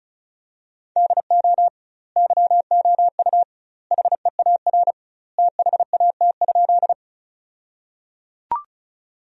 Morse code, heard as a keyed tone, about 36 wpm, 700 Hz. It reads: DO YOU HEAR THAT?
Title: 2x Word-Spacing - Sentences from Top 300 Words - 35wpm